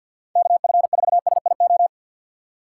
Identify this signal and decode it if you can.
Morse code, heard as a keyed tone, about 51 wpm, 700 Hz. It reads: KF4SIO
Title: USA Call Signs - 6 Characters - 4x Character Spacing - 50wpm